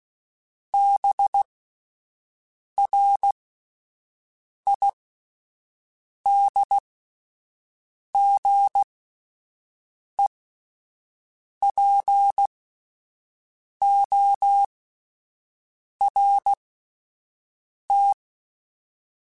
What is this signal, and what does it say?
Morse code, heard as a keyed tone, about 16 wpm, spaced as for 5 wpm, 800 Hz. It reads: BRIDGEPORT